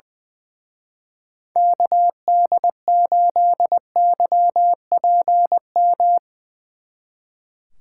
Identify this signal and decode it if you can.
Morse code, heard as a keyed tone, about 22 wpm, 700 Hz. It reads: KD8YPM
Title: Speed Racing - USA 2x3 Call Signs 20wpm